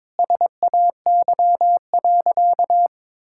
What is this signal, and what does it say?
Morse code, heard as a keyed tone, about 22 wpm, 700 Hz. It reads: SAY.
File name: practice / 22wpm / Sentences with 2-Word Phrases - 22wpm